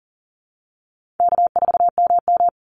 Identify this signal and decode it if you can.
Morse code, heard as a keyed tone, about 40 wpm, 700 Hz. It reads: K4MM
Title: Easy Call Signs Repeated 3x - 40wpm